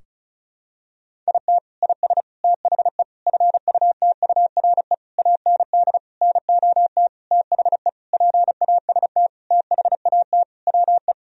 Morse code, heard as a keyed tone, about 34 wpm, 700 Hz. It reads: IT IS THE FUTURE AND NOT THE PAST THAT WE
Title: Sentences from Top 1000 Words - Review Part 4 - 35wpm